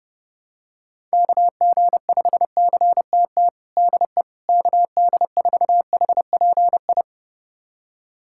Morse code, heard as a keyed tone, about 31 wpm, 700 Hz. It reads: KG5CTT DE KD4HPI
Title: USA Callsign DE Callsign 30wpm